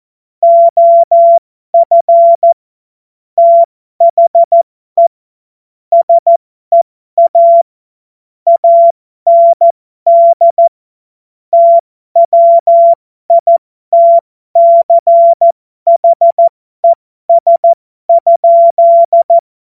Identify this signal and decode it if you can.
Morse code, heard as a keyed tone, about 14 wpm, spaced as for 12 wpm, 680 Hz. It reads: OF THE SEA AND TWITCHES?
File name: Morsekurs / Witze